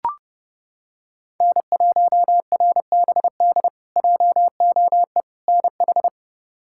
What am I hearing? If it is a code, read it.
Morse code, heard as a keyed tone, about 32 wpm, 700 Hz. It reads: N1RBD JOE NH
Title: K1USN SST QSO Practice - 30wpm